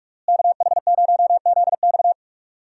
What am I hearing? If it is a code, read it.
Morse code, heard as a keyed tone, about 46 wpm, 700 Hz. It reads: KH0ZX
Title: USA Call Signs - 5 Characters - Repeated 3x - 45wpm